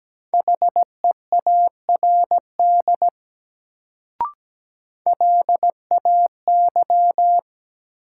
Morse code, heard as a keyed tone, about 17 wpm, 700 Hz. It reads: HEARD LAY
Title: Sentences from Top 400 Words - New Word Drill - 17wpm